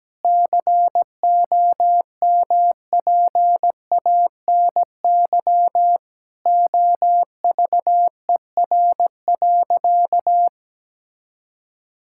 Morse code, heard as a keyed tone, about 17 wpm, 700 Hz. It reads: COMPANY OVER.
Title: Sentences from Top 1000 Words - Review Part 6 - 17wpm